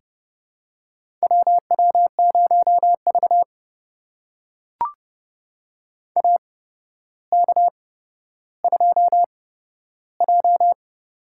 Morse code, heard as a keyed tone, about 29 wpm, spaced as for 13 wpm, 700 Hz. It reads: WW0V AK2J
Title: USA Call Signs - 4 Characters - 8x Character Spacing - 30wpm